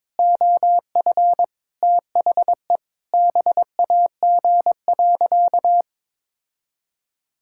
Morse code, heard as a keyed tone, about 23 wpm, 700 Hz. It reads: OF THE BAG.
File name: Sentences with 7-Word Phrases - 22wpm